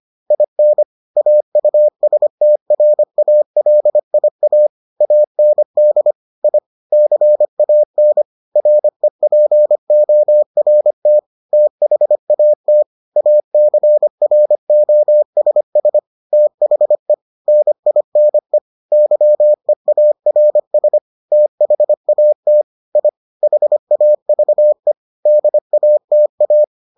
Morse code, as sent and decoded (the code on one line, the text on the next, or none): .. -. / .- ..- ... - .-. .- .-.. .. .- / .- -. -.. / .. / -.-. .- -. / .-. .--. --- .-. - / - .... .- - / .- -.-. .-. --- ... ... / - .... / -. .. -. / -.-- . .- .-. ... / - .... .- - / .. / .... .- ...- . / -.. .- - .-
IN AUSTRALIA AND I CAN REPORT THAT ACROSS THE NINE YEARS THAT I HAVE DATA